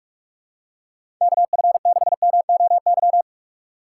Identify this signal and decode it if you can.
Morse code, heard as a keyed tone, about 46 wpm, 700 Hz. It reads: KF6MOY
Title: USA Call Signs - 6 Characters - 6x Character Spacing - 45wpm